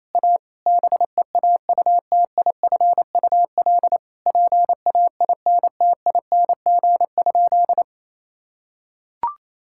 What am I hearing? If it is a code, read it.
Morse code, heard as a keyed tone, about 28 wpm, 700 Hz. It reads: A BEAUTIFUL PAINTING?
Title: Sentences from Top 1000 Words - Review Part 6 - 28wpm